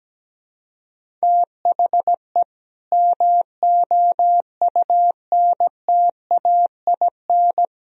Morse code, heard as a keyed tone, about 17 wpm, 700 Hz. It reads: THE MOUNTAIN
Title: Sentences from Top 1000 Words - Review Part 1 - 17wpm